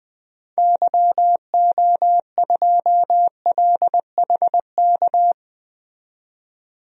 Morse code, heard as a keyed tone, about 20 wpm, 700 Hz. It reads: YO2LHK
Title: DXCC-100 Call Signs - 6 Characters - 20wpm